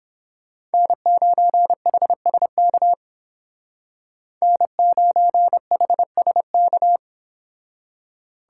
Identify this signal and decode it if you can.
Morse code, heard as a keyed tone, about 30 wpm, 700 Hz. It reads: N9HSK N9HSK